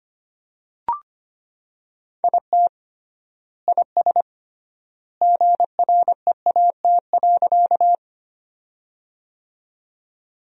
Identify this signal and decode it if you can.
Morse code, heard as a keyed tone, about 25 wpm, 700 Hz. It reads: IT IS GREAT.
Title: 3x Word-Spacing - Sentences from Top 200 Words - 25wpm